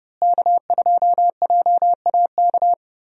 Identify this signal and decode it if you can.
Morse code, heard as a keyed tone, about 30 wpm, 700 Hz. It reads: K2JAK
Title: USA Call Signs - 5 Characters - Repeated 2x - 30wpm